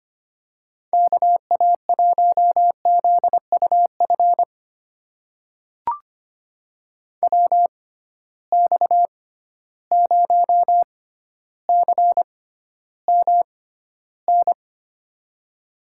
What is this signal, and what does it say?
Morse code, heard as a keyed tone, about 25 wpm, spaced as for 13 wpm, 700 Hz. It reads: KA1ZUF WX0CMN
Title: USA Call Signs - 6 Characters - 6x Character Spacing - 25wpm